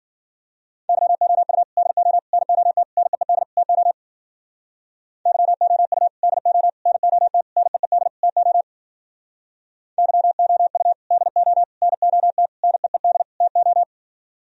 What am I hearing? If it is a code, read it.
Morse code, heard as a keyed tone, about 52 wpm, 700 Hz. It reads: YOU DO NOT NEED TO YOU DO NOT NEED TO YOU DO NOT NEED TO